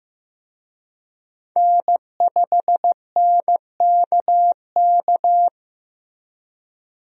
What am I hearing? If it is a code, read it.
Morse code, heard as a keyed tone, about 15 wpm, 700 Hz. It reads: N5NKK
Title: USA Call Signs - 5 Characters - Repeated 2x - 15wpm